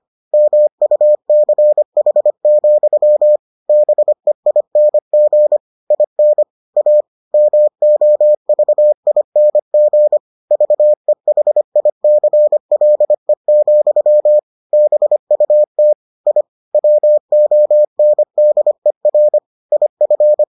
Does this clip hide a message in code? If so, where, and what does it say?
Yes: Morse code from 0.3 to 20.4 s, heard as a keyed tone, about 25 wpm, 600 Hz. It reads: MUCH, BEING IN A MOVING VEHICLE, BUT I WONDER IF